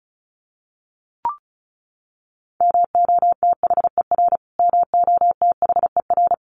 Morse code, heard as a keyed tone, about 34 wpm, 700 Hz. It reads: MOTHER MOTHER
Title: Top 300 Words Repeated 35wpm